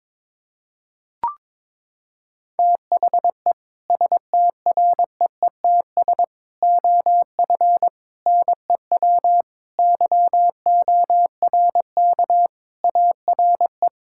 Morse code, heard as a keyed tone, about 22 wpm, 700 Hz. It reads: THE STREETS OF NEW YORK ARE
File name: practice / 22wpm / Sentences with 3-Word Phrases - 22wpm